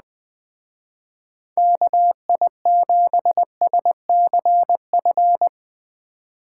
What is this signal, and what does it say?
Morse code, heard as a keyed tone, about 19 wpm, 700 Hz. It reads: KI7SCF
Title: Speed Racing - USA Call Signs 17wpm